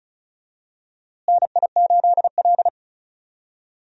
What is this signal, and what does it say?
Morse code, heard as a keyed tone, about 36 wpm, 700 Hz. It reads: NI8L
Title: USA Call Signs - 4 Characters - Repeated 3x - 35wpm